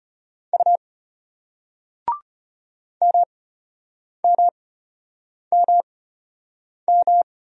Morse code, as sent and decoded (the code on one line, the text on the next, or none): ..- / -- -- -- --
U MMMM